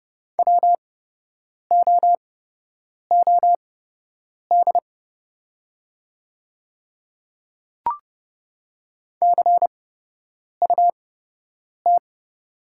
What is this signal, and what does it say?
Morse code, heard as a keyed tone, about 30 wpm, spaced as for 8 wpm, 700 Hz. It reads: WOOD CUT
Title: 8x Character Spacing - Top 300 Words - 30wpm